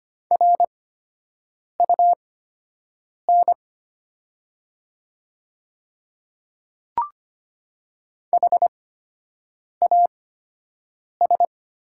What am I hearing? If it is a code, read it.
Morse code, heard as a keyed tone, about 25 wpm, spaced as for 7 wpm, 700 Hz. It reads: RUN HAS
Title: 8x Character Spacing - Top 300 Words - 25wpm